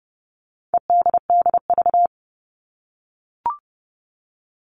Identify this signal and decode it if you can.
Morse code, heard as a keyed tone, about 30 wpm, 700 Hz. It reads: EDDV
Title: All Cut Numbers - 4-Digit 30wpm